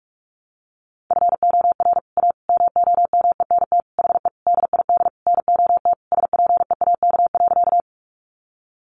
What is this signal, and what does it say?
Morse code, heard as a keyed tone, about 44 wpm, 700 Hz. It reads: FOR A MOMENT HE DID NOT SPEAK.
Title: Sentences from Top 1100 Words 45wpm